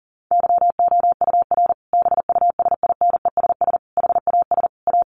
Morse code, heard as a keyed tone, about 40 wpm, 700 Hz. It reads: YOUR BUSINESS HAS A
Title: Sentences from Top 1200 Words 40wpm